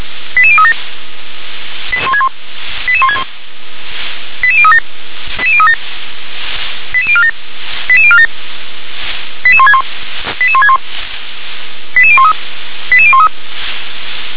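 Five-tone selective call (ZVEI), heard as quick runs of five tones, tones of about 80 ms. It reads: 88037 88161 88037 88037 88047 88047 88161 88161 88013 88013
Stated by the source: ZVEI five-tone selective call received on SDR.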